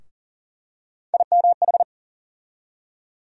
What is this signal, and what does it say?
Morse code, heard as a keyed tone, about 40 wpm, 700 Hz. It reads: IMH
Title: USA Call Sign Postfixes - 3 Letters 40wpm